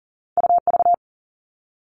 Morse code, heard as a keyed tone, about 38 wpm, 700 Hz. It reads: UV
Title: All Cut Numbers - 2-Digit 40wpm